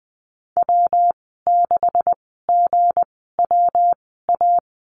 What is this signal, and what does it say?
Morse code, heard as a keyed tone, about 20 wpm, spaced as for 15 wpm, 700 Hz. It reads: W6GWA
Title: 179 Lesson 29g - Slash - Call Signs 20wpm